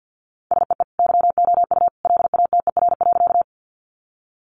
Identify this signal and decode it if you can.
Morse code, heard as a keyed tone, about 49 wpm, 700 Hz. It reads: SEE YOU LATER.
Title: Sentences from Top 700 Words 50wpm